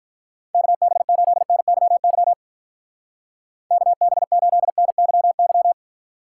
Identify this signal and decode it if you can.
Morse code, heard as a keyed tone, about 42 wpm, 700 Hz. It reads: KB8NYY KB8NYY
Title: Speed Racing - USA 2x3 Call Signs 35wpm